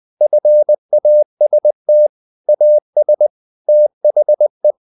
Morse code, heard as a keyed tone, about 20 wpm, 600 Hz. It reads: FAST AS THE
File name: the old man and the sea part 2 20wpm